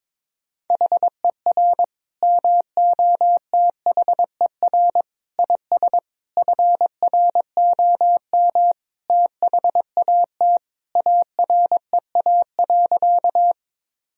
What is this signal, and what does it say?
Morse code, heard as a keyed tone, about 21 wpm, 700 Hz. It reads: HER MOTHER IS FROM THAT AREA.
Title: Sentences from Top 1000 Words - Review Part 2 - 22wpm